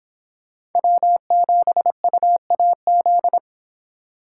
Morse code, heard as a keyed tone, about 25 wpm, 700 Hz. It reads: W7UAZ